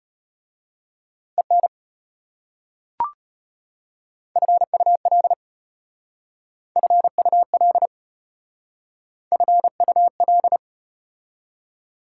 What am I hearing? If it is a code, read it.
Morse code, heard as a keyed tone, about 34 wpm, 700 Hz. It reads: EN FUL FUL FUL